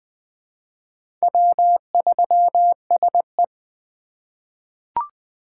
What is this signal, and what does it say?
Morse code, heard as a keyed tone, about 20 wpm, 700 Hz. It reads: W3SE
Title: USA Call Signs - 4 Characters - 3x Character Spacing - 20wpm